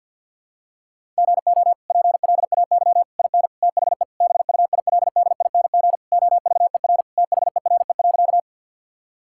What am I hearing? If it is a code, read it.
Morse code, heard as a keyed tone, about 52 wpm, 690 Hz. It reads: GO PLAY IN THE BUILDING OVER THERE.
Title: Sentences from Top 1000 Words - Review Part 5 - 50wpm